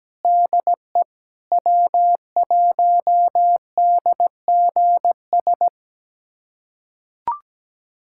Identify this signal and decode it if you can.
Morse code, heard as a keyed tone, about 17 wpm, 700 Hz. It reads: DE W1DGS
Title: USA Callsign DE Callsign - 5 Characters - 17wpm